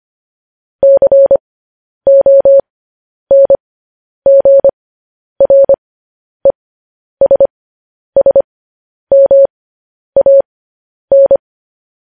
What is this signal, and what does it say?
Morse code, heard as a keyed tone, about 25 wpm, spaced as for 10 wpm, 550 Hz. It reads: CONGRESSMAN
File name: podcast / News headlines-1x-25x10